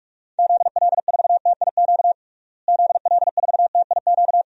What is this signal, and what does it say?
Morse code, heard as a keyed tone, about 46 wpm, 700 Hz. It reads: ZL4TIQ ZL4TIQ
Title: DXCC-100 Call Signs - 6 Characters - Repeated 3x - 45wpm